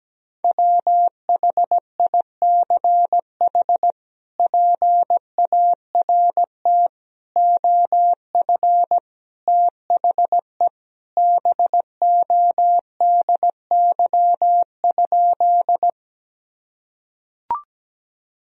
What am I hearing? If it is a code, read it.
Morse code, heard as a keyed tone, about 17 wpm, 700 Hz. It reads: WHICH PART OF THE BODY?
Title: Sentences from Top 1000 Words - Review Part 2 - 17wpm